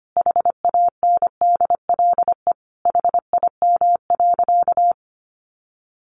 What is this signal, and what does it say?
Morse code, heard as a keyed tone, about 25 wpm, 700 Hz. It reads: HANDLE HIM.
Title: Sentences from Top 1100 Words 25wpm